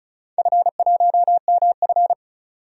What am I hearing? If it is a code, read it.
Morse code, heard as a keyed tone, about 34 wpm, 700 Hz. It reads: F1MF
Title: DXCC-100 Call Signs - 4 Characters - 35wpm